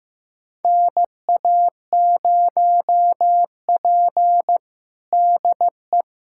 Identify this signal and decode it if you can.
Morse code, heard as a keyed tone, about 15 wpm, 700 Hz. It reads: NA0P DE
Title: USA Callsign DE Callsign - 4 Characters - 15wpm